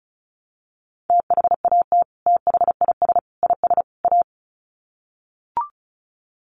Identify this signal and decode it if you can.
Morse code, heard as a keyed tone, about 34 wpm, 700 Hz. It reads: THAT THIS IS A
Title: Sets of 4 Words 35wpm